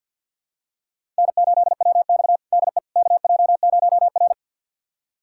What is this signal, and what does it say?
Morse code, heard as a keyed tone, about 52 wpm, 700 Hz. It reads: N8WX DE KJ0R
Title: USA Callsign DE Callsign - 4 Characters - 50wpm